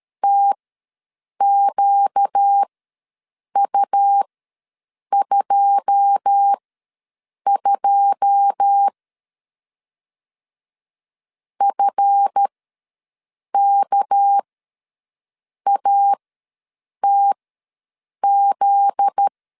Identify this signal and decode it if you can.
Morse code, heard as a keyed tone, about 13 wpm, spaced as for 7 wpm, 800 Hz. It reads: TQU22 FKATZ
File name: LEZIONI / 9 2 ws 12wpm 3406